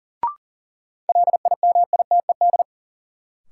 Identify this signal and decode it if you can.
Morse code, heard as a keyed tone, about 40 wpm, 700 Hz. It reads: LIMITED
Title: Sentences from Top 900 Words - New Word Drill - 40wpm